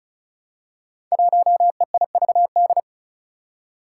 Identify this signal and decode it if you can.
Morse code, heard as a keyed tone, about 34 wpm, 700 Hz. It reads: 1EIVD
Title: Five Challenging Letter-Number Combinations 35wpm